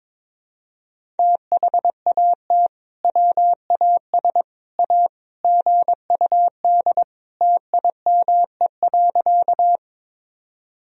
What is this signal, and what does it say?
Morse code, heard as a keyed tone, about 22 wpm, 700 Hz. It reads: THAT WAS A GUD TIME.
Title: Sentences from Top 200 Words and Abbreviations 22wpm